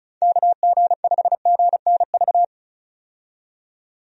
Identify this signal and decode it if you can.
Morse code, heard as a keyed tone, about 34 wpm, 700 Hz. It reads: KG5GNV
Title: USA Call Signs - 6 Characters - 35wpm